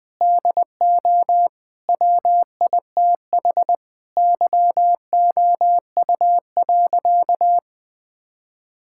Morse code, heard as a keyed tone, about 21 wpm, 700 Hz. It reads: DO WITH YOU.
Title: Sentences with 4-Word Phrases - Spoken Beginning - 20wpm